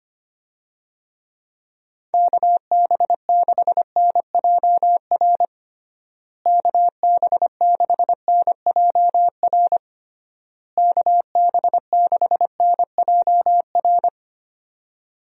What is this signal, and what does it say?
Morse code, heard as a keyed tone, about 25 wpm, 700 Hz. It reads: KB6NJR KB6NJR KB6NJR
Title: USA Call Signs - 6 Characters - Repeated 3x - 25wpm